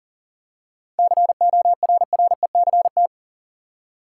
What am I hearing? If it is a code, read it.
Morse code, heard as a keyed tone, about 40 wpm, 700 Hz. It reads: CORRECT
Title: Seven Letter Words - 40wpm